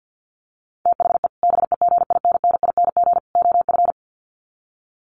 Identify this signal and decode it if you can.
Morse code, heard as a keyed tone, about 50 wpm, 700 Hz. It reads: THE BEGINNING OF
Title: Sets of 3 Words 50wpm